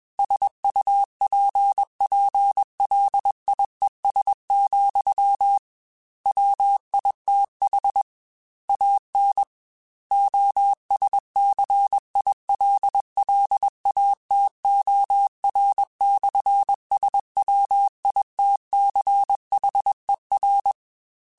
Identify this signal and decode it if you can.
Morse code, heard as a keyed tone, about 21 wpm, 800 Hz. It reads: SUPPLIES, WITH AN OSCILLATOR/SWITCHER
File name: Technical Subjects / Technical Subjects-mp3s / Sw Pwr Sup @20